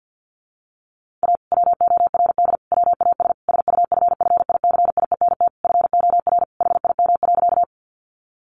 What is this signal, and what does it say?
Morse code, heard as a keyed tone, about 50 wpm, 700 Hz. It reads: A WORD WAS SUFFICIENT FOR HIM.
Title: Sentences from Top 1400 Words 50wpm